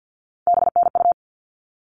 Morse code, heard as a keyed tone, about 50 wpm, 700 Hz. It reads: BNU
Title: All Cut Numbers - 3-Digit 50wpm